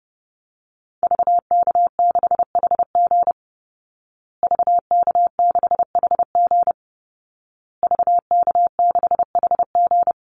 Morse code, heard as a keyed tone, about 30 wpm, 700 Hz. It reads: VK6HG VK6HG VK6HG
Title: Call Signs Repeated 2x - 30wpm